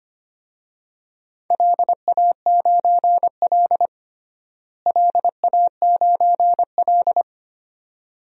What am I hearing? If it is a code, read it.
Morse code, heard as a keyed tone, about 25 wpm, 700 Hz. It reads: LA9L LA9L
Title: DXCC-100 Call Signs - 4 Characters - Repeated 2x - 25wpm